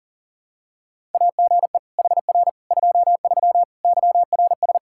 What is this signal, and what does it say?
Morse code, heard as a keyed tone, about 40 wpm, 700 Hz. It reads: AGE HR 23 YRS
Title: K1USN SST QSO Practice 40wpm